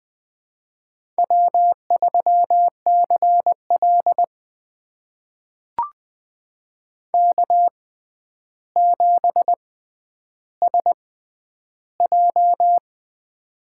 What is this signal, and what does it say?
Morse code, heard as a keyed tone, about 20 wpm, spaced as for 10 wpm, 700 Hz. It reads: W3CL K7SJ
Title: USA Call Signs - 4 Characters - 6x Character Spacing - 20wpm